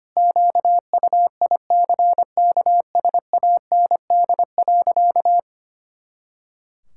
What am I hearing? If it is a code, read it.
Morse code, heard as a keyed tone, about 25 wpm, 700 Hz. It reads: QUICKSAND.
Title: Pangram Challenge v2 - 25wpm